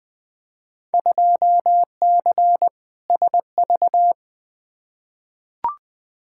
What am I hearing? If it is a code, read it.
Morse code, heard as a keyed tone, about 20 wpm, 700 Hz. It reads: 2C SV
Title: Field Day Exchanges Encore - 20wpm